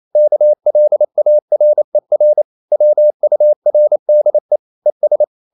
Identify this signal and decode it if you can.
Morse code, heard as a keyed tone, about 28 wpm, 600 Hz. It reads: KLARER WURDE ES
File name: Liebespaar4 28wpm